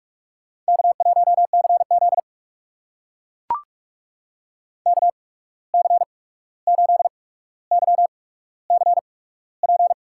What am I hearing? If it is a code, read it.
Morse code, heard as a keyed tone, about 44 wpm, spaced as for 17 wpm, 700 Hz. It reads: K1CZ KC8YCP
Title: USA Call Signs - 8x Character Spacing - 45wpm